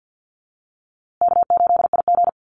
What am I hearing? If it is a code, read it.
Morse code, heard as a keyed tone, about 50 wpm, 700 Hz. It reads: K8IG
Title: USA Call Signs Repeated 3x 50wpm